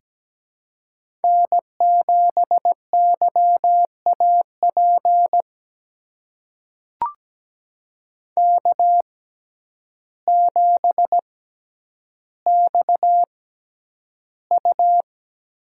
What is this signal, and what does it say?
Morse code, heard as a keyed tone, about 17 wpm, spaced as for 9 wpm, 700 Hz. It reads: N7YAP K7XU
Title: USA Call Signs - 6x Character Spacing - 17wpm